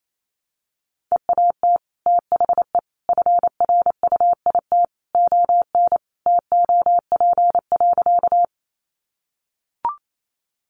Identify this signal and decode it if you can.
Morse code, heard as a keyed tone, about 28 wpm, 700 Hz. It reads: EAT THE FRUIT ON TOP.
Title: Sentences from Top 1300 Words 28wpm